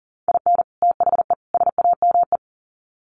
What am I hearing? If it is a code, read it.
Morse code, heard as a keyed tone, about 40 wpm, 700 Hz. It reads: IN THE SAME
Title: Sets of 3 Words 40wpm